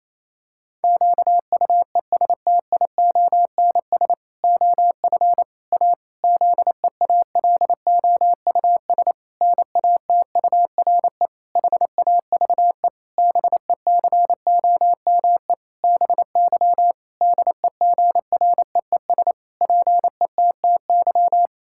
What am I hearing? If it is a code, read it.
Morse code, heard as a keyed tone, about 28 wpm, 700 Hz. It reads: QUESTIONS OF A ZEALOUS NATURE HAVE BECOME BY DEGREES PETTY
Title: Pangram Challenge v2 - 28wpm